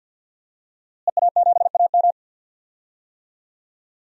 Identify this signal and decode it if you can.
Morse code, heard as a keyed tone, about 53 wpm, 700 Hz. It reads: EA7AM
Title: DXCC-100 Call Signs - 5 Characters - Repeated 2x - 50wpm